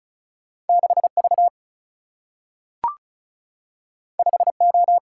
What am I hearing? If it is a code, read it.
Morse code, heard as a keyed tone, about 36 wpm, 700 Hz. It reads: 6V 5O
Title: Single-Letter Single-Number 35wpm